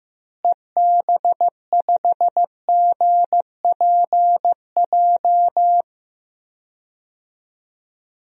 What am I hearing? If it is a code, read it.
Morse code, heard as a keyed tone, about 15 wpm, 700 Hz. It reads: EB5GPJ
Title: DXCC-100 Call Signs - 6 Characters - Repeated 2x - 15wpm